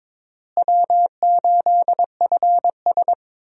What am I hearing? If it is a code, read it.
Morse code, heard as a keyed tone, about 22 wpm, 700 Hz. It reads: W8FS
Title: USA Call Signs - 4 Characters - Repeated 3x - 22wpm